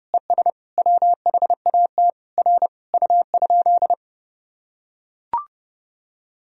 Morse code, heard as a keyed tone, about 30 wpm, 700 Hz. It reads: ES WHAT R U?
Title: Sentences from Top 100 Words and Abbreviations 30wpm